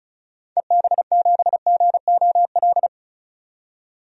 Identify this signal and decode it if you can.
Morse code, heard as a keyed tone, about 34 wpm, 700 Hz. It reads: EB7GOL
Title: DXCC-100 Call Signs - 6 Characters - 35wpm